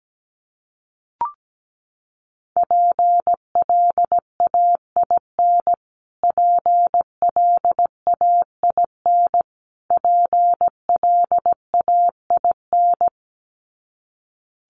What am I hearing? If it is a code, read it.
Morse code, heard as a keyed tone, about 17 wpm, 700 Hz. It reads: PLAIN PLAIN PLAIN
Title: Top 500 Words Repeated 17wpm